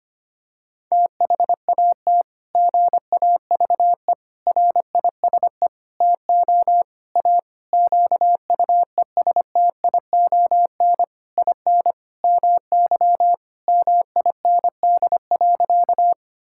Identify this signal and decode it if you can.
Morse code, heard as a keyed tone, about 25 wpm, 700 Hz. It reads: THAT GAVE RISE TO A QUESTION IN MY MIND.
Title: Sentences from Top 1000 Words - Review Part 6 - 25wpm